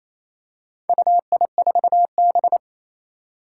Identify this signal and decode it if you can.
Morse code, heard as a keyed tone, about 27 wpm, 700 Hz. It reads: UI4B
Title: Four Challenging Letter-Number Combinations 28wpm